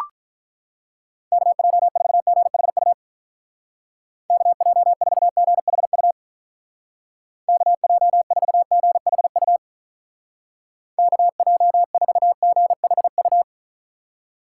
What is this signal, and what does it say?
Morse code, heard as a keyed tone, about 44 wpm, 700 Hz. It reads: KJ4GHU KJ4GHU KJ4GHU KJ4GHU